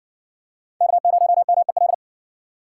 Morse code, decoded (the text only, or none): K0MEL